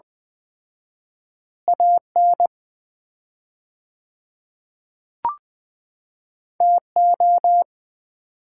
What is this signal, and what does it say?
Morse code, heard as a keyed tone, about 20 wpm, 700 Hz. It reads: AN TO
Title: 4x Word-Spacing - Sets of 2 Words - 20wpm